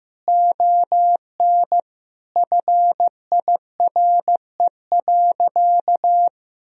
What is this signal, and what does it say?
Morse code, heard as a keyed tone, about 15 wpm, 700 Hz. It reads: ON FIRE.
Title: Sentences from Top 1000 Words - Review Part 3 - 15wpm